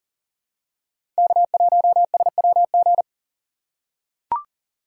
Morse code, heard as a keyed tone, about 37 wpm, 700 Hz. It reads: K1SWG
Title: USA Call Signs - 5 Characters - 2x Character Spacing - 40wpm